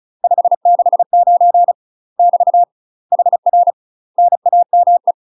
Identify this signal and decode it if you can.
Morse code, heard as a keyed tone, about 35 wpm, 700 Hz. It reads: 569 <BT> HR NAME
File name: UK MASTER QSOs TRAINING / 35WPM / UK MASTER QSOs 0011